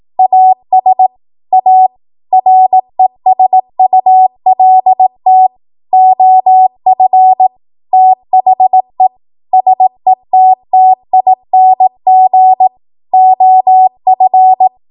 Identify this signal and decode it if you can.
Morse code, heard as a keyed tone, about 18 wpm, 750 Hz. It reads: AS A RESULT OF THE SETTING OF
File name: Morse / 18WPM